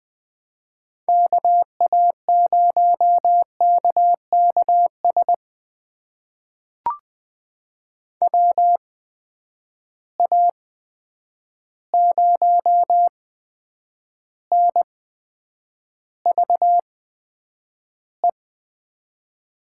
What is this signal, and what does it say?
Morse code, heard as a keyed tone, about 20 wpm, spaced as for 9 wpm, 700 Hz. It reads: KA0KKS WA0NVE
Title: USA Call Signs - 6 Characters - 8x Character Spacing - 20wpm